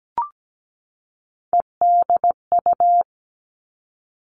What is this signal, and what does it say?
Morse code, heard as a keyed tone, about 17 wpm, 700 Hz. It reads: EDU
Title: Three Challenging Letter-Number Combinations 17wpm